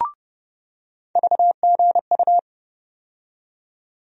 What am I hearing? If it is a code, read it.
Morse code, heard as a keyed tone, about 30 wpm, 700 Hz. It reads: VGU
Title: USA Call Sign Postfixes - 3 Letters 30wpm